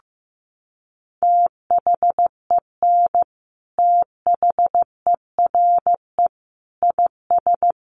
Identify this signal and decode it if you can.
Morse code, heard as a keyed tone, about 15 wpm, 700 Hz. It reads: THEN THERE IS
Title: Sentences from Top 100 Words 15wpm